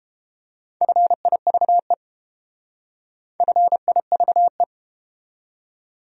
Morse code, heard as a keyed tone, about 30 wpm, 700 Hz. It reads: FIVE FIVE